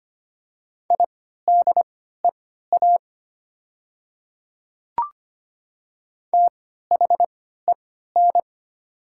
Morse code, heard as a keyed tone, about 25 wpm, spaced as for 14 wpm, 700 Hz. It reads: IDEA THEN
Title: 3x Character Spacing - Top 300 Words - 25wpm